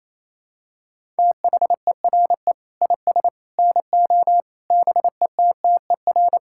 Morse code, heard as a keyed tone, about 28 wpm, 700 Hz. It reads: THERE IS NO BETTER
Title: Sentences from Top 1000 Words - Review Part 5 - 28wpm